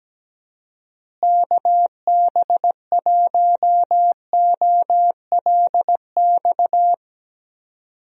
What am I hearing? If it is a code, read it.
Morse code, heard as a keyed tone, about 17 wpm, 700 Hz. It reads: KB1OLX
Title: USA Call Signs - 6 Characters - Repeated 2x - 17wpm